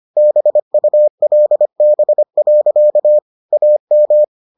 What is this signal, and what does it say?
Morse code, heard as a keyed tone, about 25 wpm, 600 Hz. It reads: BULB. AM